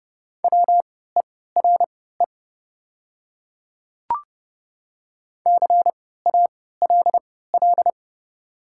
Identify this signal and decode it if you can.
Morse code, heard as a keyed tone, about 30 wpm, spaced as for 17 wpm, 700 Hz. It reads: WERE CALL
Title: 3x Character Spacing - Top 100 Words - 30wpm